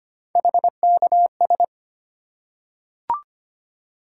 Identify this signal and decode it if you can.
Morse code, heard as a keyed tone, about 25 wpm, 700 Hz. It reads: HKS
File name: Three Letter-Number Combinations - 25wpm